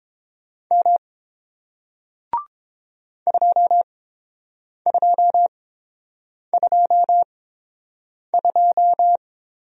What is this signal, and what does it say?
Morse code, heard as a keyed tone, about 30 wpm, spaced as for 7 wpm, 700 Hz. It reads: M 2222